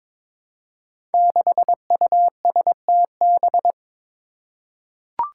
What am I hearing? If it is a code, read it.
Morse code, heard as a keyed tone, about 22 wpm, 700 Hz. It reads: 6USTB
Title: Five Challenging Letter-Number Combinations 22wpm